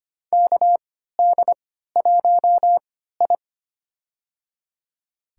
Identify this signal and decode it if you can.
Morse code, heard as a keyed tone, about 25 wpm, spaced as for 14 wpm, 700 Hz. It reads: KD1I
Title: USA Call Signs - 3x Character Spacing - 25wpm